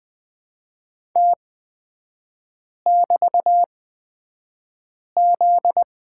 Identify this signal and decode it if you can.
Morse code, heard as a keyed tone, about 20 wpm, spaced as for 5 wpm, 700 Hz. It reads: T<BT>Z
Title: 243 Lesson 39e - Letter z - Three Characters 20wpm